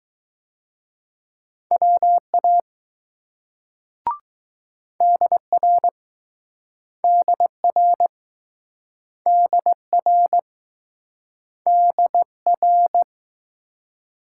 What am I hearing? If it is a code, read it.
Morse code, heard as a keyed tone, about 20 wpm, 700 Hz. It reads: WA DR DR DR DR